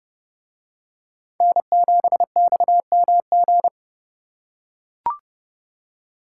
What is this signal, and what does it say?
Morse code, heard as a keyed tone, about 29 wpm, 700 Hz. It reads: N7XMG